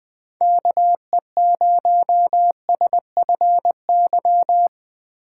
Morse code, heard as a keyed tone, about 20 wpm, 700 Hz. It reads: KE0SFY
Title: USA Call Signs - 6 Characters - 6x Character Spacing - 20wpm